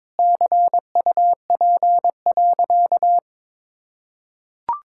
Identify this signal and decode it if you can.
Morse code, heard as a keyed tone, about 22 wpm, 700 Hz. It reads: CUP.
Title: Sentences from Top 1600 Words 22wpm